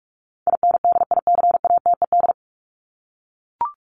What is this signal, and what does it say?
Morse code, heard as a keyed tone, about 44 wpm, 700 Hz. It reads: INDICATED
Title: ED Suffix - Top 100 Words with Base Words 45wpm